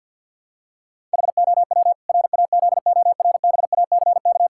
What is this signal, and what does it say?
Morse code, heard as a keyed tone, about 50 wpm, 700 Hz. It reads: HOW RAZORBACK
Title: Pangram Challenge v2 - 50wpm